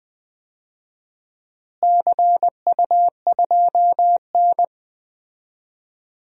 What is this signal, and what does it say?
Morse code, heard as a keyed tone, about 20 wpm, 700 Hz. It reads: CU2N
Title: DXCC-100 Call Signs - 4 Characters - Repeated 2x - 20wpm